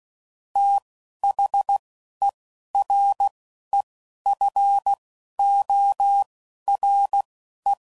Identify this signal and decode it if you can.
Morse code, heard as a keyed tone, about 16 wpm, spaced as for 11 wpm, 800 Hz. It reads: THEREFORE
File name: CW Code Groups @10WPM / Technical Subjects / AND Gate